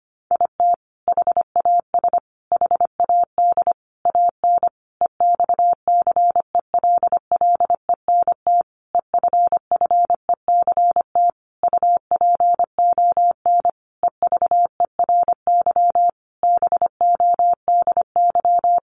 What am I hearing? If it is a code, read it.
Morse code, heard as a keyed tone, about 25 wpm, 700 Hz. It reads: IT HAS HAD AN EXCELLENT EFFECT UPON EVERY BODY
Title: Sentences from Top 1100 Words 25wpm